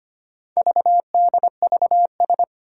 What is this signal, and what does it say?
Morse code, heard as a keyed tone, about 25 wpm, 700 Hz. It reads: VDVS
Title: Four Challenging Letter-Number Combinations 25wpm